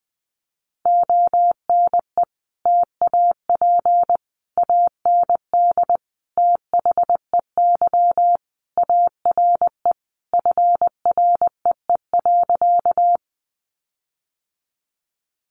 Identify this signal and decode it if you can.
Morse code, heard as a keyed tone, about 20 wpm, 700 Hz. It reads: ONE TAP AND THEY ARE FREE.
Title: Sentences from Top 1400 Words 20wpm